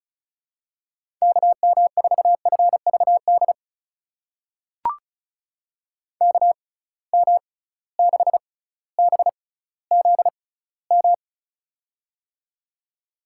Morse code, heard as a keyed tone, about 36 wpm, spaced as for 18 wpm, 700 Hz. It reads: KM4FVD KM6BZM
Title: USA Call Signs - 6 Characters - 6x Character Spacing - 35wpm